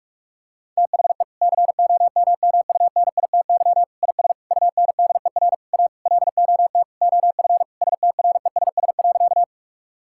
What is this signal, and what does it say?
Morse code, heard as a keyed tone, about 41 wpm, 700 Hz. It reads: THE COMMUNITY IS UNDER A LOT OF STRESS.
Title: Sentences from Top 1000 Words - Review Part 6 - 45wpm